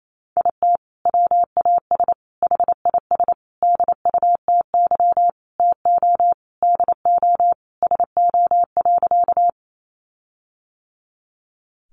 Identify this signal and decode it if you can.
Morse code, heard as a keyed tone, about 28 wpm, 700 Hz. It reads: IT WAS HIS DUTY TO DO SO.
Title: Sentences from Top 1500 Words 28wpm